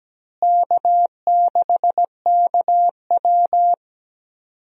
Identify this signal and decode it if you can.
Morse code, heard as a keyed tone, about 17 wpm, 700 Hz. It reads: K6KW
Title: USA Call Signs - 4 Characters - Repeated 2x - 17wpm